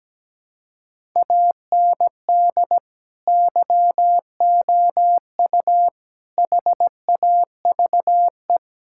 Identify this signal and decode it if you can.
Morse code, heard as a keyed tone, about 17 wpm, 700 Hz. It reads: AND YOU HAVE
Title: Sentences from Top 1000 Words - Review Part 6 - 17wpm